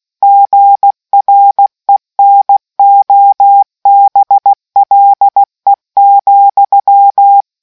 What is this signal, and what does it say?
Morse code, heard as a keyed tone, about 16 wpm, 800 Hz. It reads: GRENOBLE,